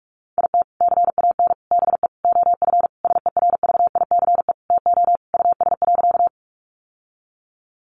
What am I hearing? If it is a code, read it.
Morse code, heard as a keyed tone, about 44 wpm, 700 Hz. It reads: IT CAN BE OF SERVICE TO US.
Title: Sentences from Top 1200 Words 45wpm